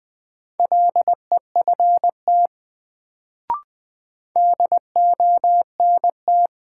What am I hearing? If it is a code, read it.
Morse code, heard as a keyed tone, about 20 wpm, 700 Hz. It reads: LEFT DONT
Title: Sentences from Top 300 Words - New Word Drill - 20wpm